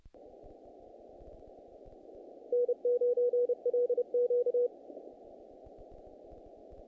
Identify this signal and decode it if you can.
Morse code, heard as a keyed tone, about 31 wpm, 490 Hz. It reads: N9LQ
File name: contests / naqp-cw-k5tr-op-k5pi- -2000